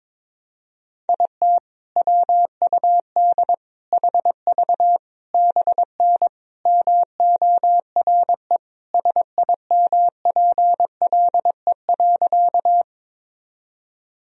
Morse code, heard as a keyed tone, about 23 wpm, 700 Hz. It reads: IT WUD HV BN MORE SIMPLE.
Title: Sentences from Top 400 Words and Abbreviations 22wpm